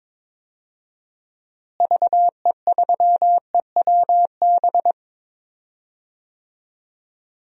Morse code, heard as a keyed tone, about 22 wpm, 700 Hz. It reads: VE3EWB
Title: DXCC-100 Call Signs - 6 Characters - Repeated 2x - 22wpm